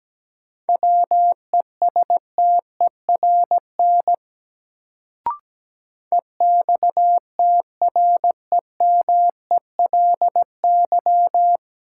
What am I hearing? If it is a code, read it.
Morse code, heard as a keyed tone, about 17 wpm, 700 Hz. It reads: WESTERN EXTREMELY
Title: Sentences from Top 1000 Words - Review Part 7 - New Word Drill - 17wpm